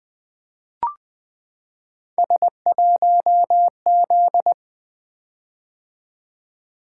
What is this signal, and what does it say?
Morse code, heard as a keyed tone, about 20 wpm, 700 Hz. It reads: S1Z
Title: Three Letter-Number Combinations - 20wpm